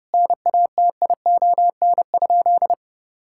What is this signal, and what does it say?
Morse code, heard as a keyed tone, about 30 wpm, 700 Hz. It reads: NATION?
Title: Sentences from Top 1000 Words - Review Part 6 - 30wpm